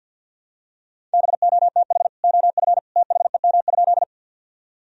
Morse code, heard as a keyed tone, about 51 wpm, 700 Hz. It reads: BOTH OF THEM?
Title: Sentences from Top 1000 Words - Review Part 1 - 50wpm